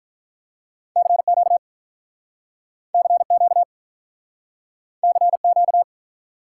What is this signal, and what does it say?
Morse code, heard as a keyed tone, about 46 wpm, 700 Hz. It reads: CQ CQ CQ